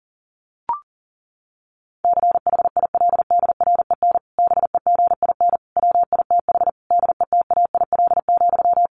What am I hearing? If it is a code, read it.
Morse code, heard as a keyed tone, about 40 wpm, 700 Hz. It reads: CHILDREN BEGIN WITH DETAIL,
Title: Sentences from Top 1500 Words 40wpm